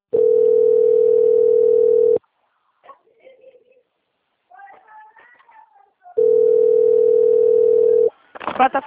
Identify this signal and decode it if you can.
Ringback tone (North America), 440 + 480 Hz.
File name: out- - - - - .38518